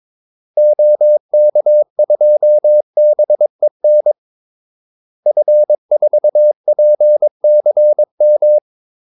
Morse code, heard as a keyed tone, about 22 wpm, 600 Hz. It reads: OK2BEN F4PCM
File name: Callsigns-22wpm